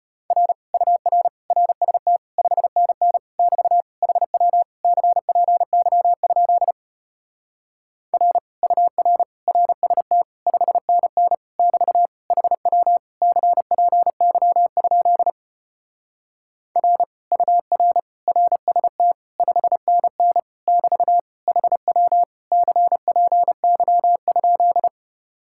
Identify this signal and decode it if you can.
Morse code, heard as a keyed tone, about 33 wpm, 700 Hz. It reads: R UR RST 5NN <BT> HW CPY? R UR RST 5NN <BT> HW CPY? R UR RST 5NN <BT> HW CPY?